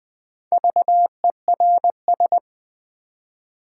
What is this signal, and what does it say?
Morse code, heard as a keyed tone, about 20 wpm, 700 Hz. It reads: VERS
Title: Build A Word - 2 Letters At A Time - 6-Letter Words - 20wpm